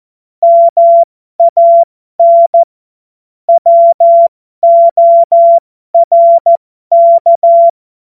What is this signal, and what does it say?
Morse code, heard as a keyed tone, about 14 wpm, spaced as for 12 wpm, 680 Hz. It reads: MAN WORK
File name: Morsekurs / Witze